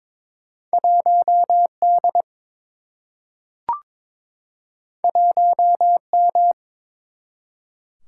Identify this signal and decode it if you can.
Morse code, heard as a keyed tone, about 23 wpm, 700 Hz. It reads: 1 D 1 M
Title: Single-Letter Single-Number 22wpm